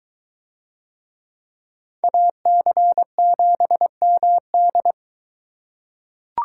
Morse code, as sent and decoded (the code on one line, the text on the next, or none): .- -.-. --... -- -..
AC7MD